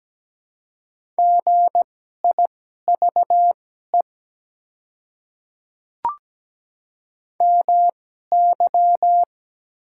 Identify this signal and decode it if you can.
Morse code, heard as a keyed tone, about 17 wpm, spaced as for 12 wpm, 700 Hz. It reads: GIVE MY
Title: 2x Character Spacing - Top 200 Words - 17wpm